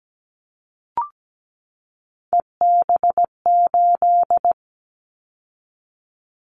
Morse code, heard as a keyed tone, about 17 wpm, 700 Hz. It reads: EB8
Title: Callsign Prefixes for Top 100 DXCC - 17wpm